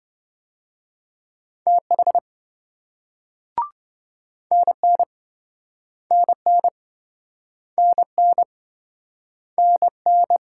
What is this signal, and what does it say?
Morse code, heard as a keyed tone, about 28 wpm, 700 Hz. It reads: TH NN NN NN NN